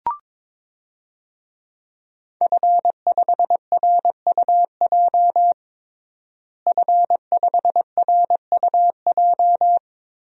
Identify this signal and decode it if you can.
Morse code, heard as a keyed tone, about 22 wpm, 700 Hz. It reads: F5RUJ F5RUJ
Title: DXCC-100 Call Signs - 5 Characters - Repeated 3x - 22wpm